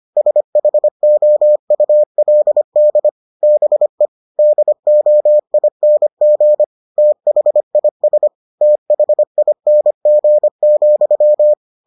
Morse code, heard as a keyed tone, about 25 wpm, 600 Hz. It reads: SHOULD BE DOING THIS THING,